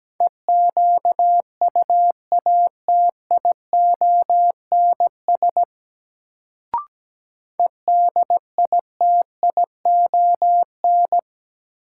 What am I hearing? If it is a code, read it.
Morse code, heard as a keyed tone, about 17 wpm, 700 Hz. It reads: EQUATIONS EDITION
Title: Sets of 5 Words - New Word Drill - 17wpm